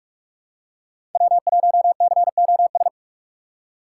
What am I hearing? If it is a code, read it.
Morse code, heard as a keyed tone, about 43 wpm, 700 Hz. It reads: W1COS
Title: USA Call Signs - 5 Characters - 6x Character Spacing - 45wpm